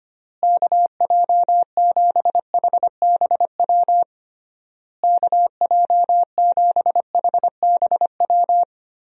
Morse code, heard as a keyed tone, about 25 wpm, 700 Hz. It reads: KJ7HBW KJ7HBW
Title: USA Call Signs - 6 Characters - Repeated 3x - 25wpm